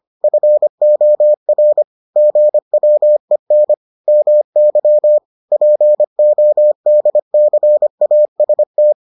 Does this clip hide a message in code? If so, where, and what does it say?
Yes: Morse code from 0.2 to 8.9 s, heard as a keyed tone, about 25 wpm, 600 Hz. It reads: FOR GWEN MY PODCAST